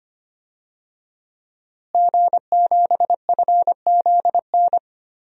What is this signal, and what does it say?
Morse code, heard as a keyed tone, about 25 wpm, 700 Hz. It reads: G7FZN
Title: DXCC-100 Call Signs - 5 Characters - Repeated 2x - 25wpm